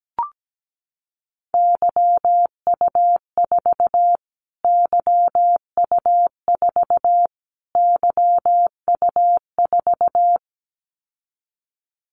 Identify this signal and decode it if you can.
Morse code, heard as a keyed tone, about 17 wpm, 700 Hz. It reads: YU4 YU4 YU4
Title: Callsign Prefixes for Top 25 DXCC - Repeated 3x - 17wpm